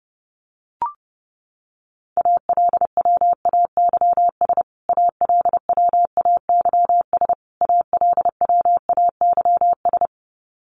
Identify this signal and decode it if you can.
Morse code, heard as a keyed tone, about 30 wpm, 700 Hz. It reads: ALWAYS ALWAYS ALWAYS
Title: Top 500 Words Repeated 30wpm